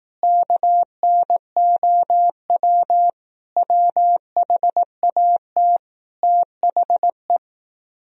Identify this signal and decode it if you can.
Morse code, heard as a keyed tone, about 18 wpm, 700 Hz. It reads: KNOW WHAT THE
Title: Speed-Racing - Sets of 5 Words 15wpm